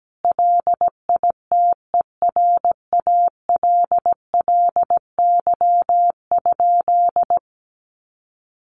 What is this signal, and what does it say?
Morse code, heard as a keyed tone, about 17 wpm, 700 Hz. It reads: LITERALLY?
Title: Sentences from Top 1400 Words 17wpm